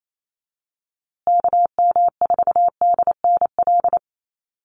Morse code, heard as a keyed tone, about 28 wpm, 700 Hz. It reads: KM4DNL
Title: USA Call Signs Repeated 3x 28wpm